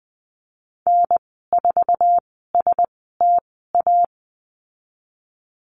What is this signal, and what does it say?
Morse code, heard as a keyed tone, about 19 wpm, spaced as for 14 wpm, 700 Hz. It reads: N4STA
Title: 048 Lesson 9g - Number 4 - Call Signs 20wpm